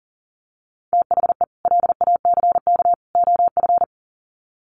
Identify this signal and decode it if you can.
Morse code, heard as a keyed tone, about 40 wpm, 700 Hz. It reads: THE LACK OF
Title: Sets of 3 Words 40wpm